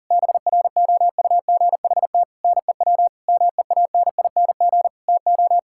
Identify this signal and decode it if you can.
Morse code, heard as a keyed tone, about 40 wpm, 700 Hz. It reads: BROUGHT NEW MEANING TO
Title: Sentences from Top 1600 Words 40wpm